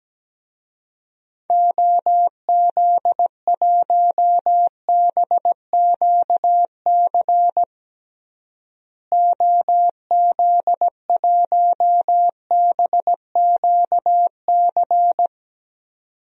Morse code, heard as a keyed tone, about 17 wpm, 700 Hz. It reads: OZ1BQC OZ1BQC
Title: DXCC-100 Call Signs - 6 Characters - Repeated 2x - 17wpm